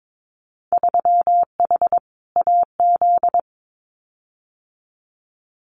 Morse code, heard as a keyed tone, about 22 wpm, 700 Hz. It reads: 3H AZ
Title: Winter Field Day Exchanges Encore 22wpm